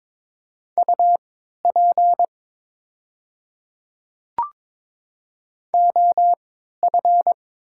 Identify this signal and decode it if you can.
Morse code, heard as a keyed tone, about 22 wpm, spaced as for 12 wpm, 700 Hz. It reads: UP OF
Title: 3x Character Spacing - Top 100 Words - 22wpm